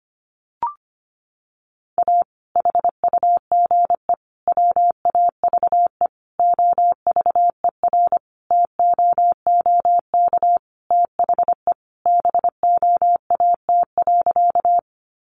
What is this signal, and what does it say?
Morse code, heard as a keyed tone, about 25 wpm, 700 Hz. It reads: A HUGE WAVE OVERTOOK THE BOAT.
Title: Sentences from Top 1400 Words 25wpm